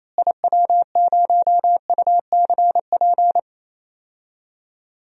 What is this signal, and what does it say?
Morse code, heard as a keyed tone, about 28 wpm, 700 Hz. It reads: IW0UCP
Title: DXCC-100 Call Signs - 6 Characters - Repeated 3x - 28wpm